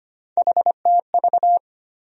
Morse code, heard as a keyed tone, about 25 wpm, 700 Hz. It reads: HTV
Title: USA Call Sign Postfixes - 3 Letters 25wpm